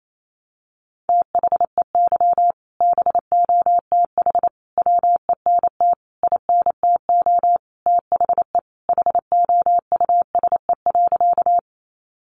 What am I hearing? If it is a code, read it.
Morse code, heard as a keyed tone, about 28 wpm, 700 Hz. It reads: THEY BOTH WENT INTO THE HOUSE.
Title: Sentences from Top 600 Words 28wpm